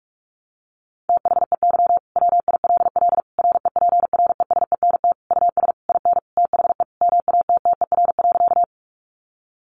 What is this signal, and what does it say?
Morse code, heard as a keyed tone, about 44 wpm, 700 Hz. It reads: THEY WILL REPRESENT US IN THE MATTER.
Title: Sentences from Top 1200 Words 45wpm